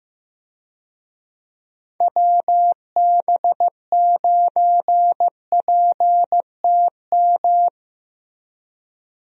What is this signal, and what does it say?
Morse code, heard as a keyed tone, about 15 wpm, 700 Hz. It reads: WB9PTM
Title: USA Call Signs - 6 Characters - Repeated 3x - 15wpm